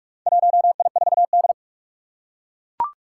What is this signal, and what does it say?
Morse code, heard as a keyed tone, about 44 wpm, 700 Hz. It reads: 1I4D
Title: Four Challenging Letter-Number Combinations 45wpm